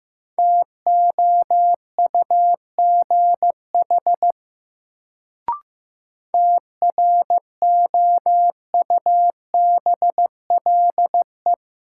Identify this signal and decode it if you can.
Morse code, heard as a keyed tone, about 15 wpm, 700 Hz. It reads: TOUGH TROUBLE
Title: Sentences from Top 1200 Words - New Word Drill - 15wpm